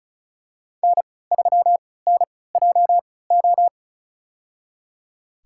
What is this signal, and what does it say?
Morse code, heard as a keyed tone, about 34 wpm, spaced as for 20 wpm, 700 Hz. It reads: N3NJO
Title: USA Call Signs - 5 Characters - 3x Character Spacing - 35wpm